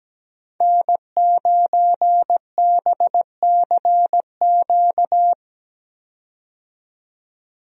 Morse code, heard as a keyed tone, about 17 wpm, 700 Hz. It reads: N9BCQ
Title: USA Call Signs - 5 Characters - 17wpm